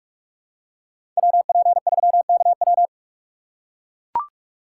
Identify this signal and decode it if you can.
Morse code, heard as a keyed tone, about 44 wpm, 700 Hz. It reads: WP3KW